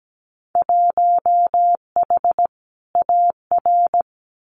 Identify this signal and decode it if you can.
Morse code, heard as a keyed tone, about 17 wpm, 700 Hz. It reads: 1H AR
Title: Winter Field Day Exchanges 17wpm